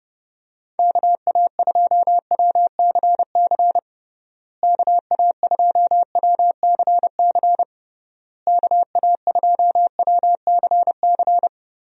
Morse code, heard as a keyed tone, about 32 wpm, 700 Hz. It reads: KA2WCC KA2WCC KA2WCC
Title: USA Call Signs - 6 Characters - Repeated 3x - 30wpm